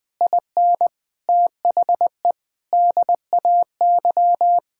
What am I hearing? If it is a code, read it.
Morse code, heard as a keyed tone, about 21 wpm, 700 Hz. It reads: IN THE DAY
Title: Sentences with 9-Word Phrases - 20wpm